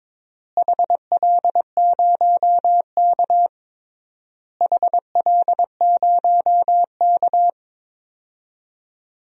Morse code, heard as a keyed tone, about 21 wpm, 700 Hz. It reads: HL0K HL0K
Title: DXCC-100 Call Signs - 4 Characters - Repeated 3x - 22wpm